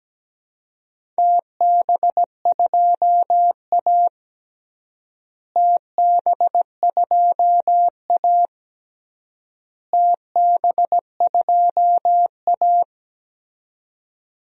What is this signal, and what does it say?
Morse code, heard as a keyed tone, about 17 wpm, 700 Hz. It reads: TB2A TB2A TB2A
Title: DXCC-100 Call Signs - 4 Characters - Repeated 3x - 17wpm